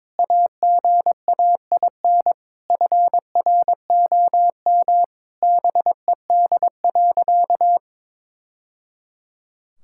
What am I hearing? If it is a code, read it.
Morse code, heard as a keyed tone, about 22 wpm, 700 Hz. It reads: AGAIN FROM BED.
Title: Sentences from Top 1000 Words - Review Part 6 - 22wpm